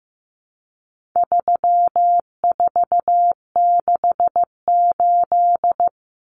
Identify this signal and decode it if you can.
Morse code, heard as a keyed tone, about 15 wpm, 700 Hz. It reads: 3468
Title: Numbers - 4-Digit 15wpm